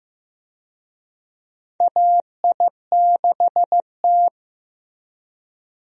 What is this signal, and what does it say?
Morse code, heard as a keyed tone, about 15 wpm, 700 Hz. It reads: AI6T
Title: USA Call Signs - 4 Characters - Repeated 3x - 15wpm